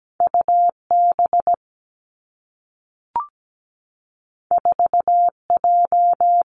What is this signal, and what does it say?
Morse code, heard as a keyed tone, about 17 wpm, 700 Hz. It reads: U B 4 J
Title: Two Challenging Letter-Number Combinations 17wpm